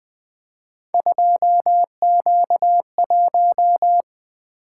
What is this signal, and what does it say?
Morse code, heard as a keyed tone, about 25 wpm, 700 Hz. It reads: 2Q1
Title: Speed-Racing - Three Letter-Number Combinations - 20wpm